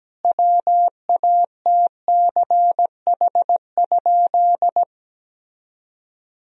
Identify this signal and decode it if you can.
Morse code, heard as a keyed tone, about 17 wpm, 700 Hz. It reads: WATCH?
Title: Sentences from Top 1000 Words - Review Part 3 - 17wpm